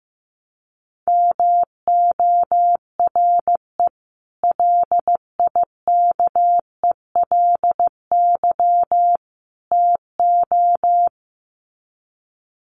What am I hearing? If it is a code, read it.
Morse code, heard as a keyed tone, about 15 wpm, 700 Hz. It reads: MORE LIKELY TO
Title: Sets of 3 Words 15wpm